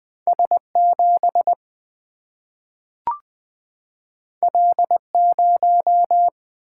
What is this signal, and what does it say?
Morse code, heard as a keyed tone, about 20 wpm, 700 Hz. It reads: S 7 L 0
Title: Single-Letter Single-Number 20wpm